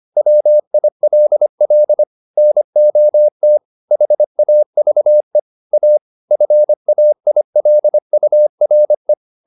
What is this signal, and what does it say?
Morse code, heard as a keyed tone, about 25 wpm, 600 Hz. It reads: WILL NOT HAVE A FAILURE
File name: the old man and the sea part 1 25wpm